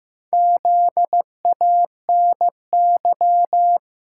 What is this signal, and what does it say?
Morse code, heard as a keyed tone, about 15 wpm, 700 Hz. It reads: ZANY
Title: Pangram Challenge v2 - 15wpm